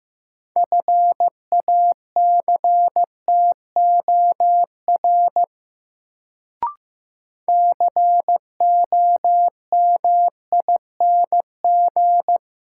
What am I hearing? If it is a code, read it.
Morse code, heard as a keyed tone, about 15 wpm, 700 Hz. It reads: FACTOR COMING
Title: Sentences from Top 1300 Words - New Word Drill Alt - 15wpm